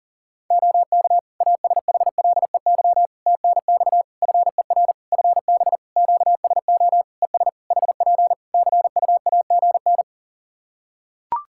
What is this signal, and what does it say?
Morse code, heard as a keyed tone, about 40 wpm, 700 Hz. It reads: OK ASHLEY TNX FER FB QSO ES HP CUAGN
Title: Practice QSO - Exchange 4 - Ending - 40wpm